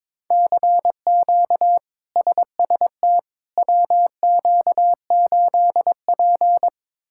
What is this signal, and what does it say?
Morse code, heard as a keyed tone, about 21 wpm, 700 Hz. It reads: CQ SST WQ8P
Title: K1USN SST QSO Practice - 22wpm